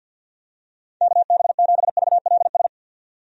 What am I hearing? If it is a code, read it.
Morse code, heard as a keyed tone, about 49 wpm, 700 Hz. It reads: KB7VLS
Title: USA Call Signs - 6 Characters - 50wpm